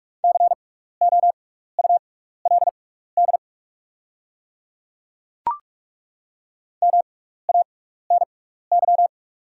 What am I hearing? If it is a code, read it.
Morse code, heard as a keyed tone, about 43 wpm, spaced as for 15 wpm, 690 Hz. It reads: COULD MANY
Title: 6x Character Spacing - Top 100 Words - 45wpm